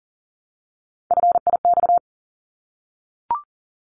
Morse code, heard as a keyed tone, about 40 wpm, 700 Hz. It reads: FIX
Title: ED Suffix - Top 100 Words with Base Words 40wpm